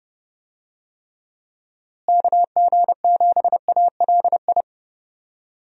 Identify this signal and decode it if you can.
Morse code, heard as a keyed tone, about 30 wpm, 700 Hz. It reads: KG7ALI
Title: USA Call Signs - 6 Characters - Repeated 2x - 30wpm